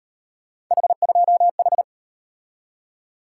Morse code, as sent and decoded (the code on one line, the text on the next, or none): .... ..--- ....
H2H